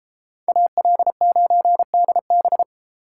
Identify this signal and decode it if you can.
Morse code, heard as a keyed tone, about 29 wpm, 700 Hz. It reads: AL9DB